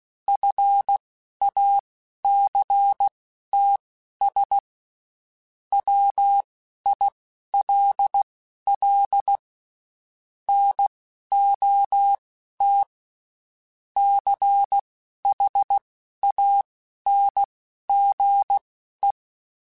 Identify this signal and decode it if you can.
Morse code, heard as a keyed tone, about 16 wpm, spaced as for 11 wpm, 800 Hz. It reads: FACTS WILL NOT CHANGE
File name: Quotes Spaced / Quotes Spaced-mp3s / IgnoringFacts sp @15